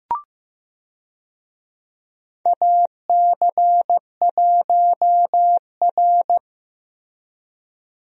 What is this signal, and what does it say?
Morse code, heard as a keyed tone, about 15 wpm, 700 Hz. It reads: AC1R
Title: USA Call Signs - 4 Characters - Repeated 3x - 15wpm